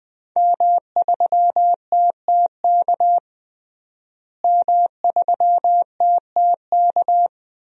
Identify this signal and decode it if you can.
Morse code, heard as a keyed tone, about 21 wpm, 700 Hz. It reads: M3TTK M3TTK
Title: DXCC-100 Call Signs - 5 Characters - Repeated 2x - 20wpm